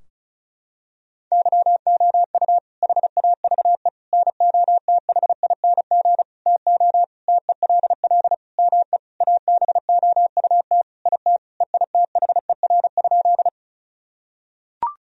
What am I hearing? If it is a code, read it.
Morse code, heard as a keyed tone, about 34 wpm, 700 Hz. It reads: YOU HAVE NOTHING TO TELL ME ABOUT IT EITHER?
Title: Sentences from Top 1000 Words - Review Part 4 - 35wpm